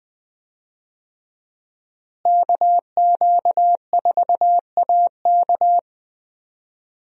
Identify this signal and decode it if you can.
Morse code, heard as a keyed tone, about 20 wpm, 700 Hz. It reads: KQ4AK
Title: USA Call Signs - 5 Characters - Repeated 2x - 20wpm